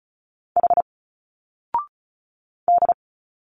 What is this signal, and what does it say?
Morse code, heard as a keyed tone, about 35 wpm, 700 Hz. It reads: H D
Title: Single Letters 35wpm